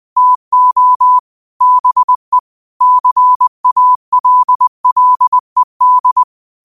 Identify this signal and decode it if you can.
Morse code, heard as a keyed tone, about 20 wpm, 1000 Hz. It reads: TO BE CALLED
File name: qotdfiles / 20wpm